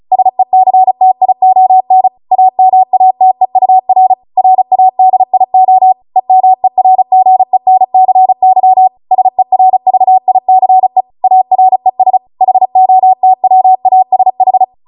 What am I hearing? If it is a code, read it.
Morse code, heard as a keyed tone, about 36 wpm, 750 Hz. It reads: SECTION AMATEUR RADIO EMERGENCY SERVICE ARES HOTWASH